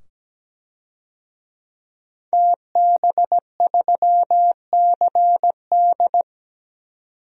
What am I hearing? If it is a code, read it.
Morse code, heard as a keyed tone, about 17 wpm, 700 Hz. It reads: TB3CD
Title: DXCC-100 Call Signs - 5 Characters - 17wpm